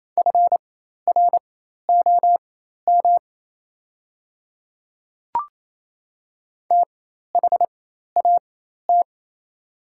Morse code, heard as a keyed tone, about 27 wpm, spaced as for 13 wpm, 700 Hz. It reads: FROM THAT
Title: 4x Character Spacing - Top 100 Words - 28wpm